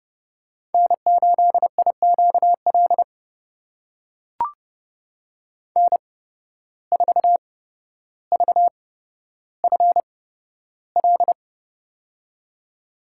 Morse code, heard as a keyed tone, about 32 wpm, spaced as for 13 wpm, 700 Hz. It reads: N8IQL N4VFL
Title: USA Call Signs - 5 Characters - 8x Character Spacing - 30wpm